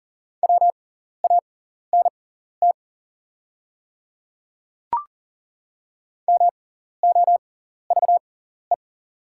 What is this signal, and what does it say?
Morse code, heard as a keyed tone, about 40 wpm, spaced as for 14 wpm, 690 Hz. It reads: WANT MOVE
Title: 6x Character Spacing - Top 200 Words - 40wpm